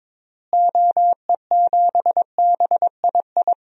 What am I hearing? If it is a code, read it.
Morse code, heard as a keyed tone, about 22 wpm, 700 Hz. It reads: OE7BII
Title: DXCC-100 Call Signs - 6 Characters - Repeated 3x - 22wpm